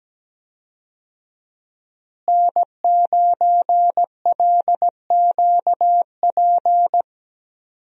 Morse code, heard as a keyed tone, about 17 wpm, 700 Hz. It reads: N9LQP
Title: USA Call Signs - 5 Characters - Repeated 3x - 17wpm